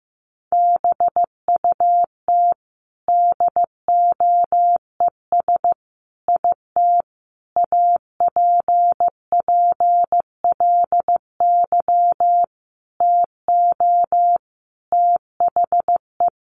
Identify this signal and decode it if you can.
Morse code, heard as a keyed tone, about 15 wpm, 700 Hz. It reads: BUT DOES IT APPLY TO THE
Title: Sentences from Top 1200 Words 15wpm